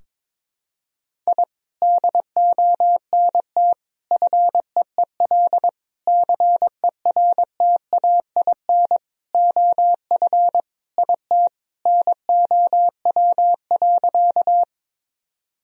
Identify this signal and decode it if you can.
Morse code, heard as a keyed tone, about 22 wpm, 700 Hz. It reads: I DONT FEEL CERTAIN OF IT NOW.
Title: Sentences from Top 1000 Words - Review Part 2 - 22wpm